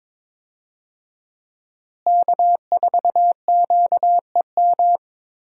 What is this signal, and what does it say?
Morse code, heard as a keyed tone, about 22 wpm, 700 Hz. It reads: K4QEM
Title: USA Call Signs - 5 Characters - Repeated 2x - 22wpm